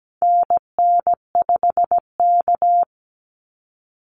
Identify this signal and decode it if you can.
Morse code, heard as a keyed tone, about 17 wpm, 700 Hz. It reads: NN5K
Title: NA SOTA Activator Callsigns 17wpm